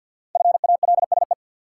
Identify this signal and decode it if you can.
Morse code, heard as a keyed tone, about 53 wpm, 700 Hz. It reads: FALSE
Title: Sentences from Top 1600 Words - New Word Drill - 50wpm